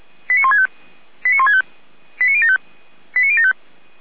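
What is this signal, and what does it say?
Five-tone selective call (ZVEI), heard as quick runs of five tones, tones of about 70 ms. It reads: 89165 89165 89075 89075
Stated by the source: ZVEI five-tone call received on an SDR.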